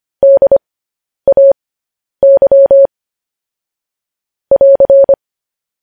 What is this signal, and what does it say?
Morse code, heard as a keyed tone, about 25 wpm, spaced as for 10 wpm, 550 Hz. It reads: DAY <AR>
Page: News Headlines in Morse Code at 10 WPM